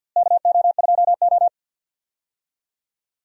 Morse code, heard as a keyed tone, about 49 wpm, 700 Hz. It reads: KO2O
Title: USA Call Signs - 4 Characters - Repeated 3x - 50wpm